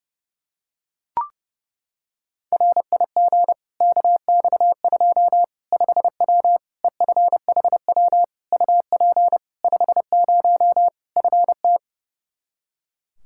Morse code, heard as a keyed tone, about 30 wpm, 700 Hz. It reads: RIG KX2 5W EFHW UP 50 FT
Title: Practice QSO - Exchange 2 - Info - 30wpm